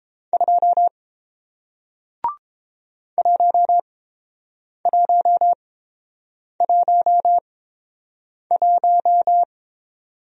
Morse code, heard as a keyed tone, about 30 wpm, spaced as for 7 wpm, 700 Hz. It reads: 2 1111